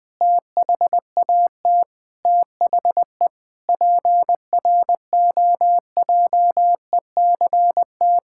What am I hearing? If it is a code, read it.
Morse code, heard as a keyed tone, about 19 wpm, 700 Hz. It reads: THAT THE PROJECT
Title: Sentences with 8-Word Phrases - 20wpm